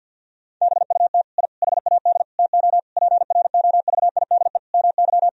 Morse code, heard as a keyed tone, about 46 wpm, 700 Hz. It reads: BUT I HAD TO PROVIDE MY
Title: Sentences from Top 1000 Words - Review Part 3 - 50wpm